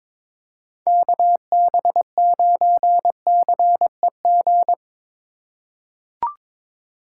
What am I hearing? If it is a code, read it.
Morse code, heard as a keyed tone, about 22 wpm, 700 Hz. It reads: KB9CEG
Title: USA Call Signs - 8x Character Spacing - 22wpm